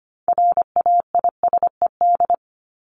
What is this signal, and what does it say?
Morse code, heard as a keyed tone, about 25 wpm, 700 Hz. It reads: RAISED
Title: ED Suffix - Top 100 Words with Base Words 25wpm